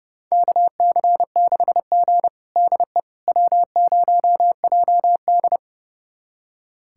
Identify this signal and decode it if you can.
Morse code, heard as a keyed tone, about 32 wpm, 700 Hz. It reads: KC6G DE W0JD
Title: USA Callsign DE Callsign - 4 Characters - 30wpm